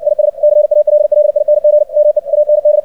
Morse code, heard as a keyed tone, about 60 wpm, 600 Hz. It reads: IS CALLING ME AND
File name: archived / CRSnet